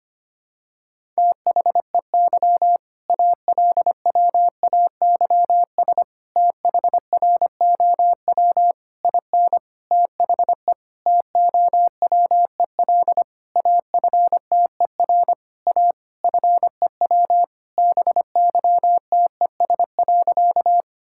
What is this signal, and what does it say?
Morse code, heard as a keyed tone, about 25 wpm, 700 Hz. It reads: THEY ALWAYS THROW IN THE TOWEL AFTER A FEW BYTES.
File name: practice / 25wpm / All-Age Appropriate Jokes - 25wpm